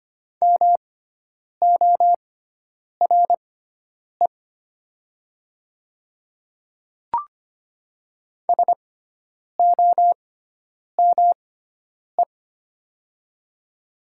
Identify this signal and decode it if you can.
Morse code, heard as a keyed tone, about 25 wpm, spaced as for 9 wpm, 700 Hz. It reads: MORE SOME
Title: 6x Character Spacing - Top 100 Words - 25wpm